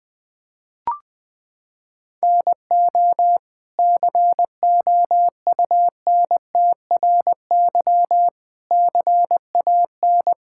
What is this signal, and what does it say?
Morse code, heard as a keyed tone, about 20 wpm, 700 Hz. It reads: NO COUNTRY CAN
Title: Sentences from Top 1000 Words - Review Part 5 - 20wpm